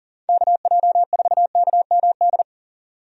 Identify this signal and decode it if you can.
Morse code, heard as a keyed tone, about 40 wpm, 700 Hz. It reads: KJ4KMD
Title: USA Call Signs - 6 Characters - 40wpm